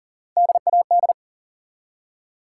Morse code, heard as a keyed tone, about 42 wpm, 700 Hz. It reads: DAD
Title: Three Letter Words - 40wpm